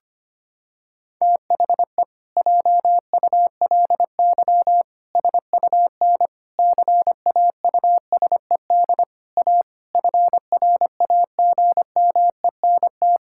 Morse code, heard as a keyed tone, about 25 wpm, 700 Hz. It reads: THE JULY SUN CAUSED A FRAGMENT
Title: Pangram Challenge v2 - 25wpm